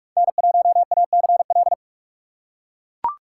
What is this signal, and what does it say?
Morse code, heard as a keyed tone, about 43 wpm, 700 Hz. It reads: N1ACL
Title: USA Call Signs - 3x Character Spacing - 45wpm